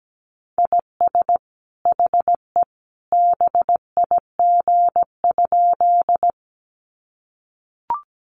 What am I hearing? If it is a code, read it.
Morse code, heard as a keyed tone, about 17 wpm, 700 Hz. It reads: IS HE BIG?
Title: Sentences from Top 200 Words 17wpm